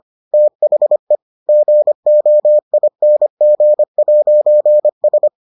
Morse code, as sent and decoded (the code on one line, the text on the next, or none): - .... / --. --- .. -. --. .----. ...
THE GOING'S